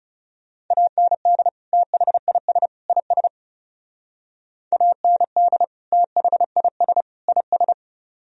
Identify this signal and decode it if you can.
Morse code, heard as a keyed tone, about 36 wpm, 700 Hz. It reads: AND THIS IS AND THIS IS
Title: Speed-Racing - Sets of 3 Words 30wpm